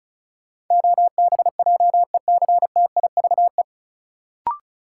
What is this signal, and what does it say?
Morse code, heard as a keyed tone, about 34 wpm, 700 Hz. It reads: OBJECTIVE
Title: Sentences from Top 1200 Words - New Word Drill - 35wpm